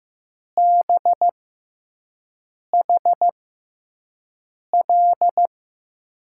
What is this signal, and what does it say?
Morse code, heard as a keyed tone, about 15 wpm, spaced as for 5 wpm, 700 Hz. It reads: BHL
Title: USA Call Signs - 6 Characters - 6x Character Spacing - 15wpm